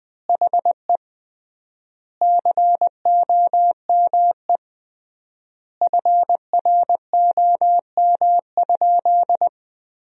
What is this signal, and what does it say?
Morse code, heard as a keyed tone, about 20 wpm, 700 Hz. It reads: HE COME FROM?
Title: 3x Word-Spacing - Sentences from Top 200 Words - 20wpm